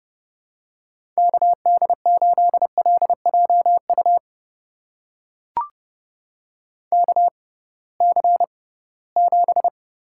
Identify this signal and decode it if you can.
Morse code, heard as a keyed tone, about 30 wpm, spaced as for 19 wpm, 700 Hz. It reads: KD8LJU KC7
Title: USA Call Signs - 6x Character Spacing - 30wpm